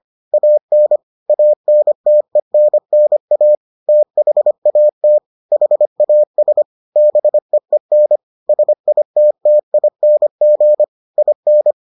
Morse code, heard as a keyed tone, about 25 wpm, 600 Hz. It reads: AN ANTENNA THAT HAS BEEN SITTING IN